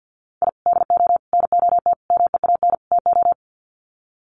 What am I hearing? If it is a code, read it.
Morse code, heard as a keyed tone, about 50 wpm, 700 Hz. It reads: I DO NOT MEAN TO
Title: Sets of 5 Words 50wpm